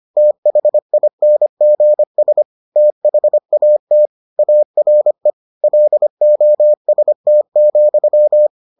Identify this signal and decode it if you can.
Morse code, heard as a keyed tone, about 25 wpm, 600 Hz. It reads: THINGS THAT ARE LOST,